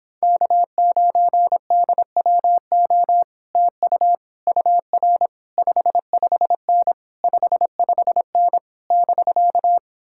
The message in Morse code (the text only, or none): K9DWO TU UR 55N 55N <BK>